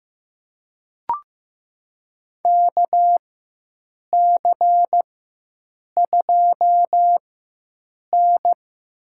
Morse code, heard as a keyed tone, about 15 wpm, spaced as for 7 wpm, 700 Hz. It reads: KC2N
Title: USA Call Signs - 6 Characters - 4x Character Spacing - 15wpm